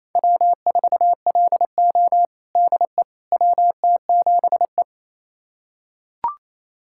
Morse code, heard as a keyed tone, about 28 wpm, 700 Hz. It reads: W4LO DE WT7E
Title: USA Callsign DE Callsign - 4 Characters - 28wpm